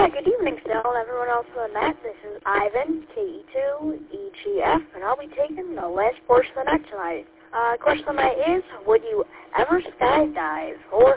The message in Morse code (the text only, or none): none